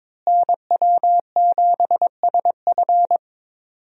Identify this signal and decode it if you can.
Morse code, heard as a keyed tone, about 22 wpm, 700 Hz. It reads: NW7SF
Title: USA Call Signs - 5 Characters - Repeated 3x - 22wpm